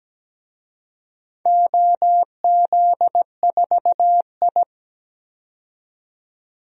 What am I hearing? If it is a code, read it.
Morse code, heard as a keyed tone, about 17 wpm, 700 Hz. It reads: OZ4I
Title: DXCC-100 Call Signs - 4 Characters - 17wpm